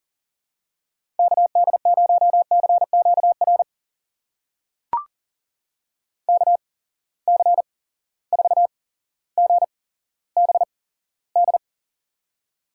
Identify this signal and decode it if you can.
Morse code, heard as a keyed tone, about 40 wpm, spaced as for 17 wpm, 700 Hz. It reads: KD0CQR KC4GBD
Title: USA Call Signs - 6 Characters - 8x Character Spacing - 40wpm